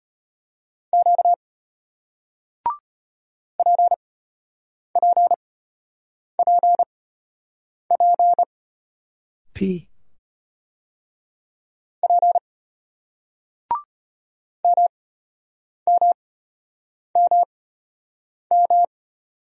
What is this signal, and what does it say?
Morse code, heard as a keyed tone, about 34 wpm, spaced as for 8 wpm, 700 Hz. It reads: Q PPPP P MMMM